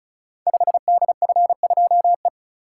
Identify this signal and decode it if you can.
Morse code, heard as a keyed tone, about 36 wpm, 690 Hz. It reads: 5DF2E
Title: Code Groups - 5 Letter:Numbers - 35wpm